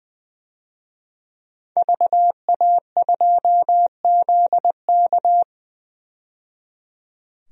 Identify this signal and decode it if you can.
Morse code, heard as a keyed tone, about 20 wpm, 700 Hz. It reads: VA2ZK
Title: DXCC-100 Call Signs - 5 Characters - 20wpm